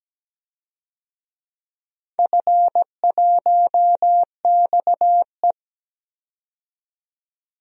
Morse code, heard as a keyed tone, about 17 wpm, 700 Hz. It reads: F1XE
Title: DXCC-100 Call Signs - 4 Characters - 17wpm